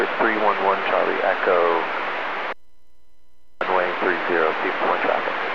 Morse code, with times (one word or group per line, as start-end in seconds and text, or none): none